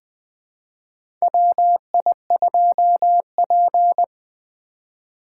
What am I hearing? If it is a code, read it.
Morse code, heard as a keyed tone, about 21 wpm, 700 Hz. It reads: WI2P
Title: USA Call Signs - 4 Characters - 4x Character Spacing - 20wpm